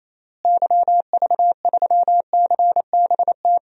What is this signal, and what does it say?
Morse code, heard as a keyed tone, about 28 wpm, 700 Hz. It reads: YV3CBT
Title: DXCC-100 Call Signs - 6 Characters - Repeated 3x - 28wpm